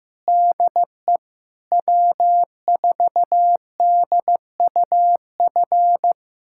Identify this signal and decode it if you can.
Morse code, heard as a keyed tone, about 15 wpm, 700 Hz. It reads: DE W4DUF
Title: USA Callsign DE Callsign - 5 Characters - 15wpm